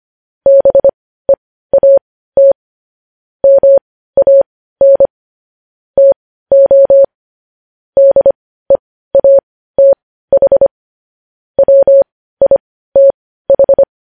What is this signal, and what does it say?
Morse code, heard as a keyed tone, about 25 wpm, spaced as for 15 wpm, 550 Hz. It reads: BEAT MAN TO DEATH WITH
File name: podcast / News headlines-1x-25x15